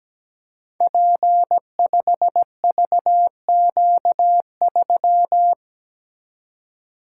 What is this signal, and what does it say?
Morse code, heard as a keyed tone, about 17 wpm, 700 Hz. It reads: P5VQ3
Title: Code Groups - 5 Letter:Numbers - 17wpm